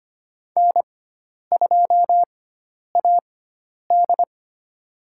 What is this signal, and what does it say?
Morse code, heard as a keyed tone, about 25 wpm, spaced as for 10 wpm, 700 Hz. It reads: N2AD
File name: Lesson 16g - Letter c - Call Signs 25wpm